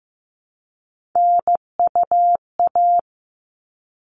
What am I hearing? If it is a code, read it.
Morse code, heard as a keyed tone, about 15 wpm, 700 Hz. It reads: NUA
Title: All Cut Numbers - 3-Digit 15wpm